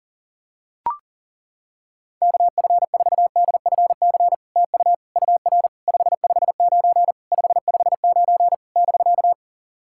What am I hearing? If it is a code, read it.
Morse code, heard as a keyed tone, about 40 wpm, 700 Hz. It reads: KF4DFC TU UR 559 559 <BK>
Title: SOTA QSO Practice 40wpm